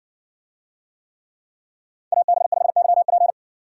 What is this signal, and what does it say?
Morse code, heard as a keyed tone, about 60 wpm, 700 Hz. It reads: AB5YP